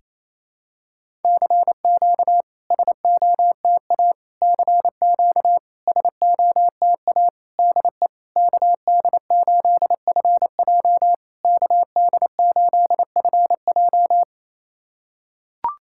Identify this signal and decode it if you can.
Morse code, heard as a keyed tone, about 28 wpm, 700 Hz. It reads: CQ SOTA CQ SOTA DE KD8FJ KD8FJ
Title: SOTA QSO Practice 28wpm